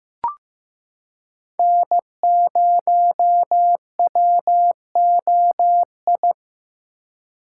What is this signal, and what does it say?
Morse code, heard as a keyed tone, about 15 wpm, 700 Hz. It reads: N0WOI
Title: K1USN SST QSO Practice - 15wpm